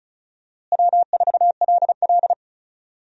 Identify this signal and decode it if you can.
Morse code, heard as a keyed tone, about 36 wpm, 700 Hz. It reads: W4LL
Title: USA Call Signs - 4 Characters - Repeated 3x - 35wpm